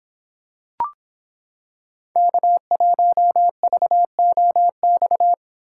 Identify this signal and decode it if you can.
Morse code, heard as a keyed tone, about 26 wpm, 700 Hz. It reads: K1VOX